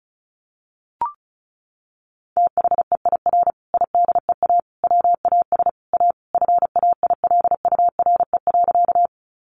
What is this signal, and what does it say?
Morse code, heard as a keyed tone, about 34 wpm, 700 Hz. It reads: THEIR IDEA WAS A FAILURE.
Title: Sentences from Top 900 Words 35wpm